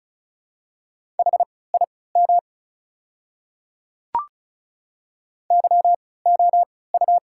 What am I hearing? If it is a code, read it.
Morse code, heard as a keyed tone, about 36 wpm, spaced as for 20 wpm, 690 Hz. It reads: HIM YOU
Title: 3x Character Spacing - Top 200 Words - 35wpm